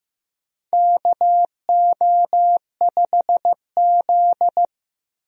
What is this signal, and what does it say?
Morse code, heard as a keyed tone, about 15 wpm, 700 Hz. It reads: KO5Z
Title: USA Call Signs - 4 Characters - Repeated 3x - 15wpm